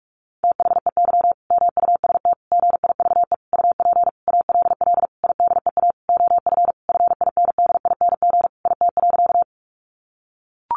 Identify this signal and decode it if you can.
Morse code, heard as a keyed tone, about 44 wpm, 700 Hz. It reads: THEY MUST GIVE UP ALL IDEA OF FINDING IT.
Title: Sentences from Top 1100 Words 45wpm